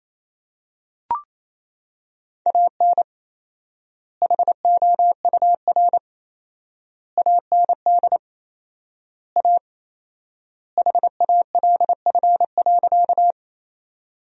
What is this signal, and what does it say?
Morse code, heard as a keyed tone, about 28 wpm, 700 Hz. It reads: AN HOUR AND A HALF.
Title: 4x Word-Spacing - Sentences from Top 400 Words - 28wpm